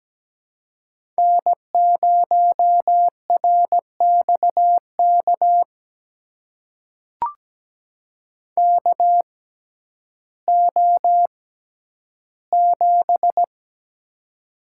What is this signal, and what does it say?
Morse code, heard as a keyed tone, about 17 wpm, spaced as for 10 wpm, 700 Hz. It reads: N0RXK KO7
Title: USA Call Signs - 6x Character Spacing - 17wpm